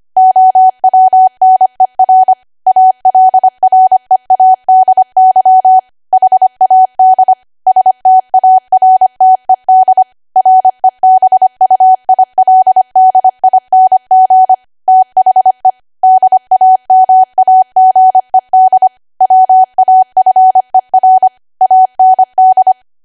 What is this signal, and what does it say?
Morse code, heard as a keyed tone, about 25 wpm, 750 Hz. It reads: OWNER ALREADY HAD STARTED REBUILDING THE DAMAGED WAFER AND
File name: Archive / 25 WPM / 25WPM